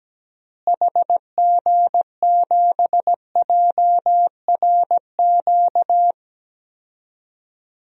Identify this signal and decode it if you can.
Morse code, heard as a keyed tone, about 17 wpm, 700 Hz. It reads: HG7JRQ
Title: DXCC-100 Call Signs - 6 Characters - Repeated 2x - 17wpm